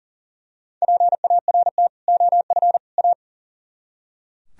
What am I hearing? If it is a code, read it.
Morse code, heard as a keyed tone, about 35 wpm, 700 Hz. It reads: PART OF A
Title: Speed-Racing - Sets of 3 Words 40wpm